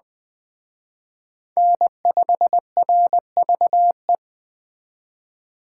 Morse code, heard as a keyed tone, about 19 wpm, 700 Hz. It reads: N5RVE